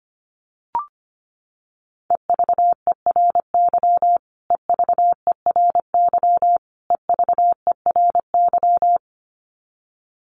Morse code, heard as a keyed tone, about 25 wpm, 700 Hz. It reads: EVERY EVERY EVERY
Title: Top 300 Words Repeated 25wpm